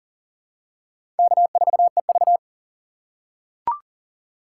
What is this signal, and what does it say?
Morse code, heard as a keyed tone, about 37 wpm, 700 Hz. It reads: K4EV
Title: USA Call Signs - 4 Characters - 2x Character Spacing - 40wpm